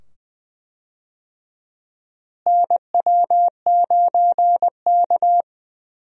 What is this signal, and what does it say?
Morse code, heard as a keyed tone, about 20 wpm, 700 Hz. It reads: NW9K
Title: USA Call Signs - 4 Characters - Repeated 2x - 20wpm